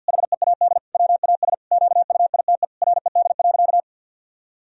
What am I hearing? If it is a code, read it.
Morse code, heard as a keyed tone, about 50 wpm, 700 Hz. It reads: HEAD WAS QUITE RED.
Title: Sentences from Top 1000 Words - Review Part 2 - 50wpm